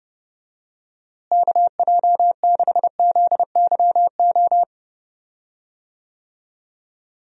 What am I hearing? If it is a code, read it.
Morse code, heard as a keyed tone, about 29 wpm, 700 Hz. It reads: KJ6ZYO
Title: USA Call Signs - 6 Characters - Repeated 3x - 30wpm